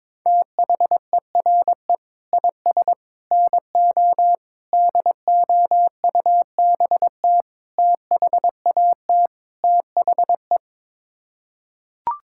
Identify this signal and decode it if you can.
Morse code, heard as a keyed tone, about 22 wpm, 700 Hz. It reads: THERE IS NO DOUBT THAT THE
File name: practice / 22wpm / Sets of 6 Words - 22wpm